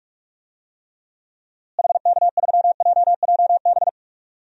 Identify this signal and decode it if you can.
Morse code, heard as a keyed tone, about 46 wpm, 700 Hz. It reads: HK3JJB
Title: DXCC-100 Call Signs - 6 Characters - Repeated 2x - 45wpm